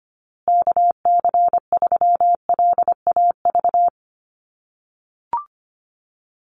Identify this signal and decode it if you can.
Morse code, heard as a keyed tone, about 25 wpm, 700 Hz. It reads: KC3LAV
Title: USA Call Signs Repeated 3x 25wpm